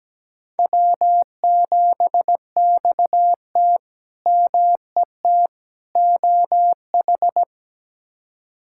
Morse code, heard as a keyed tone, about 17 wpm, 700 Hz. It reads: W7XT MET OH
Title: K1USN SST QSO Practice - 17wpm